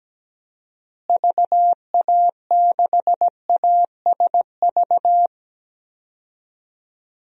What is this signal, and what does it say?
Morse code, heard as a keyed tone, about 17 wpm, 700 Hz. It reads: VA6ASV
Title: DXCC-100 Call Signs - 6 Characters - Repeated 2x - 17wpm